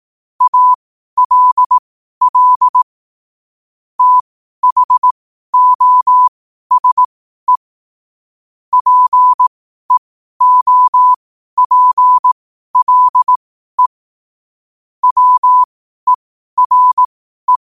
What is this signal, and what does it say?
Morse code, heard as a keyed tone, about 18 wpm, spaced as for 12 wpm, 1000 Hz. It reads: ALL THOSE PEOPLE WERE
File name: qotdfiles / 12wpm